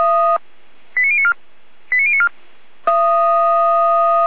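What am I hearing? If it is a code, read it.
Five-tone selective call (ZVEI), heard as quick runs of five tones, tones of about 80 ms. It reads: 89094 89094
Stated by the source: ZVEI five-tone selective call received on SDR.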